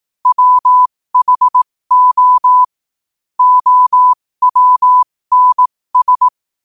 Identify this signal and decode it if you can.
Morse code, heard as a keyed tone, about 18 wpm, spaced as for 15 wpm, 1000 Hz. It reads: WHO OWNS
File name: qotdfiles / 15wpm